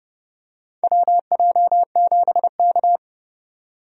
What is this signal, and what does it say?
Morse code, heard as a keyed tone, about 32 wpm, 700 Hz. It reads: WJ7K
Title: USA Call Signs - 4 Characters - 30wpm